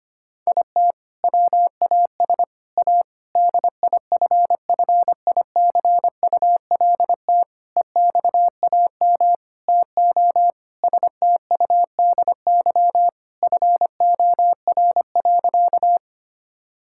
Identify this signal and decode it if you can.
Morse code, heard as a keyed tone, about 25 wpm, 700 Hz. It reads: IT WAS A DIFFICULT EXAM TO STUDY FOR.
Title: Sentences from Top 1000 Words - Review Part 5 - 25wpm